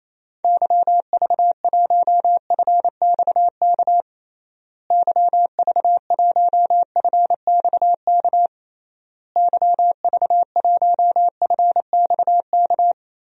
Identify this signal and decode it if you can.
Morse code, heard as a keyed tone, about 27 wpm, 700 Hz. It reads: YV1FXK YV1FXK YV1FXK
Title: DXCC-100 Call Signs - 6 Characters - Repeated 3x - 28wpm